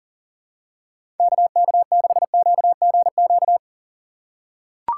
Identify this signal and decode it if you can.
Morse code, heard as a keyed tone, about 39 wpm, 700 Hz. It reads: KK6QGQ
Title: USA Call Signs - 6 Characters - 6x Character Spacing - 40wpm